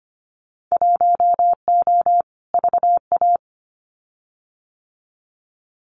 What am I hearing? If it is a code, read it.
Morse code, heard as a keyed tone, about 25 wpm, 700 Hz. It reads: 1O VA
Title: Winter Field Day Exchanges 25wpm